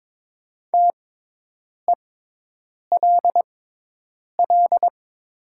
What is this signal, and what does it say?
Morse code, heard as a keyed tone, about 22 wpm, spaced as for 8 wpm, 700 Hz. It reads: TELL
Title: 6x Character Spacing - Top 200 Words - 22wpm